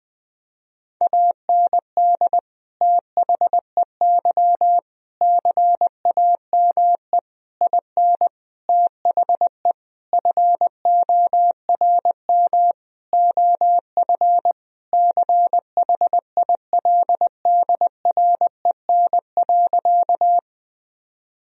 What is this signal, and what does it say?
Morse code, heard as a keyed tone, about 20 wpm, 700 Hz. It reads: AND THEY CAME IN THE FORM OF CHILDREN.
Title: Sentences from Top 1000 Words - Review Part 1 - 20wpm